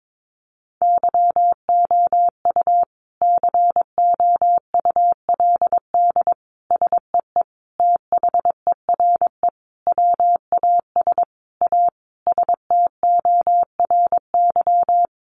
Morse code, heard as a keyed tone, about 22 wpm, 700 Hz. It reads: YOU COULD SEE THERE WAS A STORY
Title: Sentences from Top 500 Words 22wpm